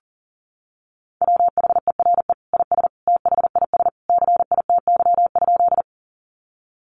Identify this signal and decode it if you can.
Morse code, heard as a keyed tone, about 40 wpm, 700 Hz. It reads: WHERE IS THIS CITY?
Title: Sentences from Top 300 Words 40wpm